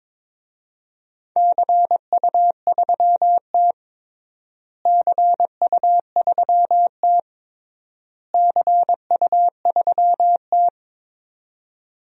Morse code, heard as a keyed tone, about 22 wpm, 700 Hz. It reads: CU3T CU3T CU3T
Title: DXCC-100 Call Signs - 4 Characters - Repeated 3x - 22wpm